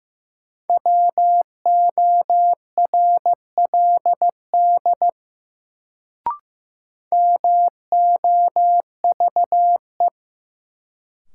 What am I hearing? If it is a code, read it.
Morse code, heard as a keyed tone, about 15 wpm, 700 Hz. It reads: WORLD MOVE
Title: Sentences from Top 200 Words - New Word Drill - 15wpm